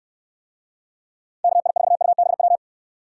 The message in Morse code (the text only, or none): KI4UBW